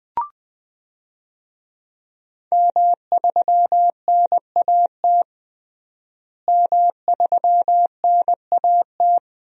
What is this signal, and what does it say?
Morse code, heard as a keyed tone, about 20 wpm, 700 Hz. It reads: M3NAT M3NAT
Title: DXCC-100 Call Signs - 5 Characters - Repeated 2x - 20wpm